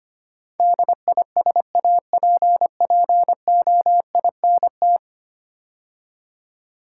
Morse code, heard as a keyed tone, about 25 wpm, 690 Hz. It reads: DISAPPOINT
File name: 10 Letter Words - First 3 Letters Given - 25wpm